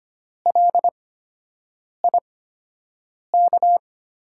Morse code, heard as a keyed tone, about 25 wpm, spaced as for 7 wpm, 700 Hz. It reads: LIK
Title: USA Call Signs - 5 Characters - 8x Character Spacing - 25wpm